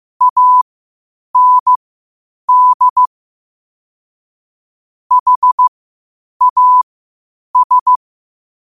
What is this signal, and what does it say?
Morse code, heard as a keyed tone, about 15 wpm, spaced as for 8 wpm, 1000 Hz. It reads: AND HAS
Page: Quotes for Fri, 15 Aug 2025 in Morse Code at 8 words per minute.